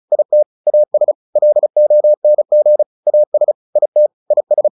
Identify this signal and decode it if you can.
Morse code, heard as a keyed tone, about 34 wpm, 600 Hz. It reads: IT AS LONG AS IT IS
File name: the old man and the sea 2 35wpm